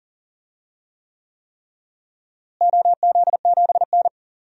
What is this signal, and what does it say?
Morse code, heard as a keyed tone, about 40 wpm, 700 Hz. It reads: OZ7N
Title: DXCC-100 Call Signs - 4 Characters - Repeated 3x - 40wpm